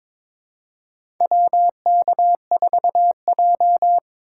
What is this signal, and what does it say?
Morse code, heard as a keyed tone, about 22 wpm, 700 Hz. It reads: WK4J
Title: USA Call Signs - 2x Character Spacing - 22wpm